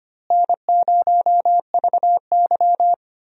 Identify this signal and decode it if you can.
Morse code, heard as a keyed tone, about 25 wpm, 700 Hz. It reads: N0VY
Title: USA Call Signs - 4 Characters - Repeated 2x - 25wpm